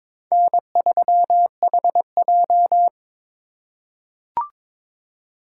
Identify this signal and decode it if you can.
Morse code, heard as a keyed tone, about 22 wpm, 700 Hz. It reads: N3HJ
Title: USA Call Signs - 4x Character Spacing - 22wpm